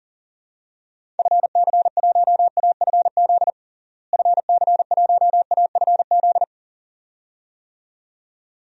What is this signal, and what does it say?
Morse code, heard as a keyed tone, about 37 wpm, 700 Hz. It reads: FC1AFZ FC1AFZ
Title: DXCC-100 Call Signs - 6 Characters - Repeated 2x - 40wpm